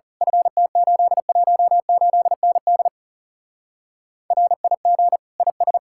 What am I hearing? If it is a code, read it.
Morse code, heard as a keyed tone, about 34 wpm, 700 Hz. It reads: FT818ND RIG IS
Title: Speed-Racing Common QSO Phrases - 30wpm